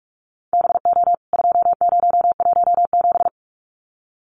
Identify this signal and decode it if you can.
Morse code, heard as a keyed tone, about 44 wpm, 700 Hz. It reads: BO 2017
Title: CWT Exchange 45wpm